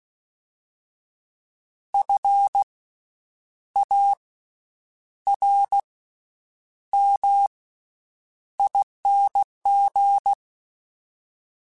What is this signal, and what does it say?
Morse code, heard as a keyed tone, about 16 wpm, spaced as for 8 wpm, 800 Hz. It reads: FARMING
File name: Special INGs-mp3s / Special INGs-mp3s / Slow ing @15